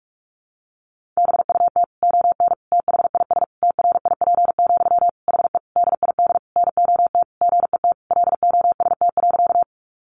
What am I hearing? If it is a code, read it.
Morse code, heard as a keyed tone, about 44 wpm, 700 Hz. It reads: BUT ON THIS TRIP, HE DID NOT GET LOST.
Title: Sentences from Top 1400 Words 45wpm